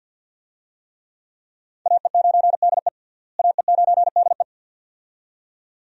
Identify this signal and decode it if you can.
Morse code, heard as a keyed tone, about 47 wpm, 700 Hz. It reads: AE9DE AE9DE
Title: USA Call Signs - 5 Characters - Repeated 2x - 50wpm